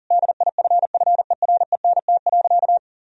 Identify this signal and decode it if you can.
Morse code, heard as a keyed tone, about 37 wpm, 700 Hz. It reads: DIFFERENT.
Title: Sentences from Top 1000 Words - Review Part 3 - 40wpm